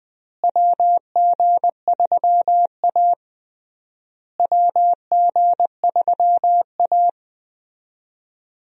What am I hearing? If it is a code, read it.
Morse code, heard as a keyed tone, about 20 wpm, 700 Hz. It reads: WG3A WG3A
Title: USA Call Signs - 4 Characters - Repeated 3x - 20wpm